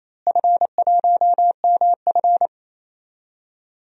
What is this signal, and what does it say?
Morse code, heard as a keyed tone, about 28 wpm, 700 Hz. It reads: F1MF
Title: DXCC-100 Call Signs - 4 Characters - Repeated 2x - 28wpm